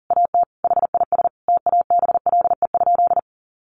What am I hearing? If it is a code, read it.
Morse code, heard as a keyed tone, about 40 wpm, 700 Hz. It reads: AT HIS TABLE?
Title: Sentences from Top 1500 Words 40wpm